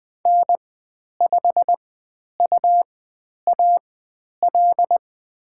Morse code, heard as a keyed tone, about 20 wpm, spaced as for 10 wpm, 700 Hz. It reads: N5UAL
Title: 102 Lesson 17g - Letter u - Call Signs 20wpm